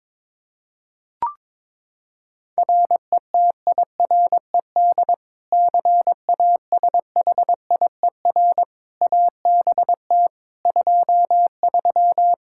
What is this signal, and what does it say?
Morse code, heard as a keyed tone, about 22 wpm, 700 Hz. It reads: RETIRED CASHIER ABT 23
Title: Practice QSO - Exchange 3 - More Info - 22wpm